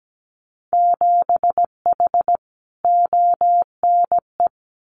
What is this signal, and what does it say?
Morse code, heard as a keyed tone, about 17 wpm, 700 Hz. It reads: 7H ONE
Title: Winter Field Day Exchanges Encore 17wpm